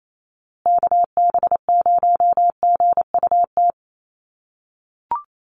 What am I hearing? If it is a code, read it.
Morse code, heard as a keyed tone, about 28 wpm, 700 Hz. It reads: KB0GUT
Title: USA Call Signs Repeated 2x 28wpm